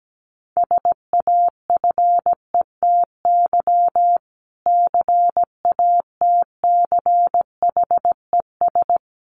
Morse code, heard as a keyed tone, about 17 wpm, 700 Hz. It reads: SAFETY CATCHES
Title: A Sound Of Thunder 17wpm